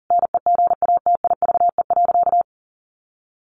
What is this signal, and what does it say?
Morse code, heard as a keyed tone, about 40 wpm, 700 Hz. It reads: NEGATIVE.
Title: Sentences from Top 1000 Words 40wpm